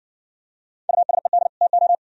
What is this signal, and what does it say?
Morse code, heard as a keyed tone, about 60 wpm, 700 Hz. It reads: USED TO